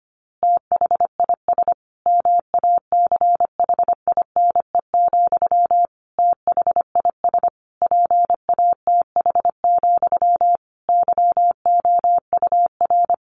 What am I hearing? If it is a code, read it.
Morse code, heard as a keyed tone, about 25 wpm, 700 Hz. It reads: THIS MACHINE, THIS PATH, YOUR
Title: A Sound Of Thunder 25wpm